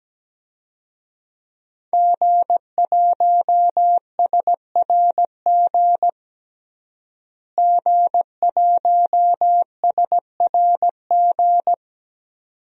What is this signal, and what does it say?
Morse code, heard as a keyed tone, about 17 wpm, 700 Hz. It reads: G1SRG G1SRG
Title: DXCC-100 Call Signs - 5 Characters - Repeated 2x - 17wpm